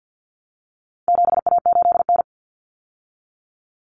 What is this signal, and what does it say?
Morse code, heard as a keyed tone, about 50 wpm, 700 Hz. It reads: 7A8N
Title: Common Cut Numbers - 4-Digit 50wpm